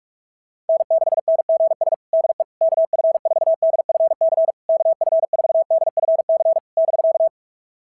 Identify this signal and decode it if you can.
Morse code, heard as a keyed tone, about 46 wpm, 660 Hz. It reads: N6NGS DE KF4DFC KF4DFC <BK>
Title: SOTA QSO Practice 45wpm